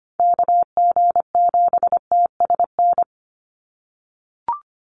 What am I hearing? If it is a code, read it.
Morse code, heard as a keyed tone, about 25 wpm, 700 Hz. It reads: KG7TSN
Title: USA Call Signs 25wpm